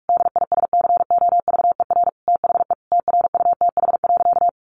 Morse code, heard as a keyed tone, about 44 wpm, 700 Hz. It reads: DISCOVER THE TRUTH.
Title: Sentences from Top 1100 Words 45wpm